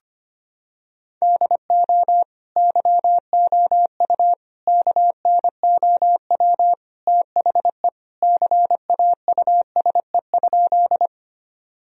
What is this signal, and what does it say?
Morse code, heard as a keyed tone, about 25 wpm, 700 Hz. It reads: DO YOU KNOW THE CAUSE?
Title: Sentences from Top 1000 Words - Review Part 1 - 25wpm